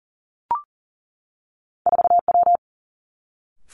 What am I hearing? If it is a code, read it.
Morse code, heard as a keyed tone, about 39 wpm, 700 Hz. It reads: 4W
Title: Two Challenging Letter-Number Combinations 40wpm